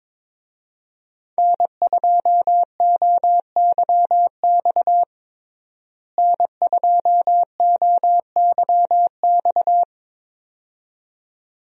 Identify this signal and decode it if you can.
Morse code, heard as a keyed tone, about 22 wpm, 700 Hz. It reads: N2OYX N2OYX
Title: USA Call Signs - 5 Characters - Repeated 2x - 22wpm